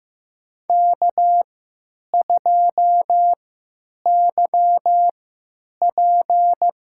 Morse code, heard as a keyed tone, about 15 wpm, spaced as for 8 wpm, 700 Hz. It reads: K2YP
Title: USA Call Signs - 3x Character Spacing - 15wpm